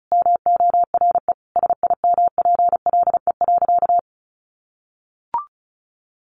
Morse code, heard as a keyed tone, about 34 wpm, 700 Hz. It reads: MORE SIMPLE.
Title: Sentences from Top 400 Words 35wpm